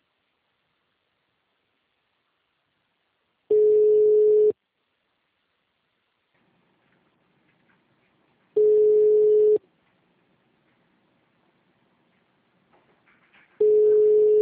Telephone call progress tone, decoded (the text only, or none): ringback tone